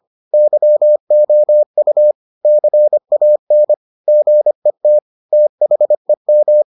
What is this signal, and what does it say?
Morse code, heard as a keyed tone, about 25 wpm, 600 Hz. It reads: YOU CAN GET THEM